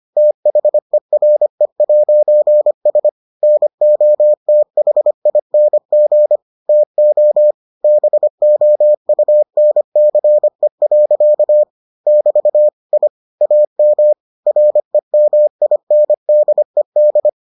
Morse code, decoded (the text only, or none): THERE'S NOTHING TO BOUNCE. <BT> I AM REMINDED